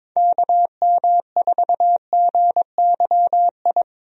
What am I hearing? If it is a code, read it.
Morse code, heard as a keyed tone, about 22 wpm, 700 Hz. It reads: KM4GYI
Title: USA Call Signs - 6 Characters - Repeated 2x - 22wpm